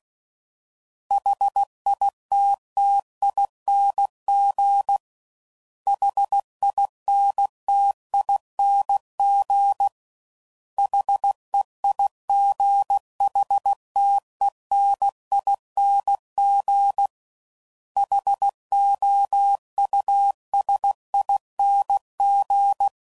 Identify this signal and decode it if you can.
Morse code, heard as a keyed tone, about 16 wpm, 800 Hz. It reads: HITTING HINTING HEIGHTENING HOUSING
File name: Suffixes / Special INGs / Sngl Ings